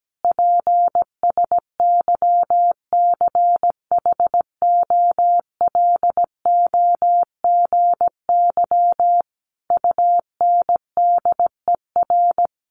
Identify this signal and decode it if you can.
Morse code, heard as a keyed tone, about 17 wpm, 700 Hz. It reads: PSYCHOLOGY UNDER
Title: Sentences from Top 1400 Words 17wpm